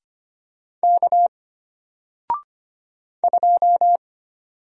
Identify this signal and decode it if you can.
Morse code, heard as a keyed tone, about 25 wpm, 700 Hz. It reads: K 2
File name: Single-Letter-Number-25wpm